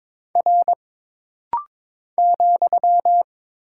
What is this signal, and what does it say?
Morse code, heard as a keyed tone, about 22 wpm, 700 Hz. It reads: R ,
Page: This rapid-fire timing is geared to push your ICR (Instant Character Recognition) skills.